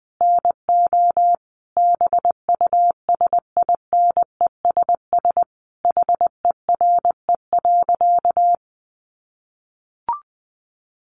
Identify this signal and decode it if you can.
Morse code, heard as a keyed tone, about 19 wpm, 700 Hz. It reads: NO BUSINESS HERE.
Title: Sentences from Top 1300 Words 20wpm